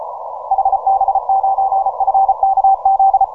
Morse code, heard as a keyed tone, about 36 wpm, 780 Hz. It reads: HB9FKZ